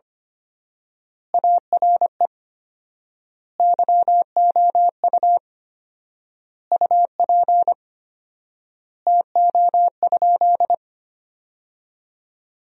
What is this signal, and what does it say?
Morse code, heard as a keyed tone, about 25 wpm, 700 Hz. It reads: ARE YOU UP TO?
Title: 4x Word-Spacing - Sentences from Top 100 Words - 25wpm